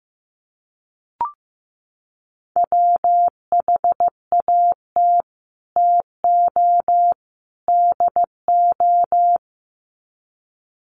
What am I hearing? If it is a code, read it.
Morse code, heard as a keyed tone, about 15 wpm, 700 Hz. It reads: WHAT TO DO
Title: Sets of 3 Words 15wpm